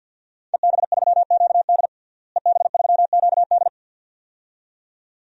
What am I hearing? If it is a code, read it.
Morse code, heard as a keyed tone, about 51 wpm, 700 Hz. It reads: EB3QD EB3QD
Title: DXCC-100 Call Signs - 5 Characters - Repeated 2x - 50wpm